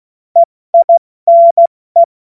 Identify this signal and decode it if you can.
Morse code, heard as a keyed tone, about 16 wpm, 680 Hz. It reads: EINE
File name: Morsekurs / Witze / Witz 05. April 2022 70 F80